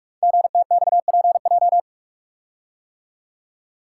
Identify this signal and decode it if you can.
Morse code, heard as a keyed tone, about 44 wpm, 700 Hz. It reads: GTXPJ
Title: Code Groups - 5 Letters - 45wpm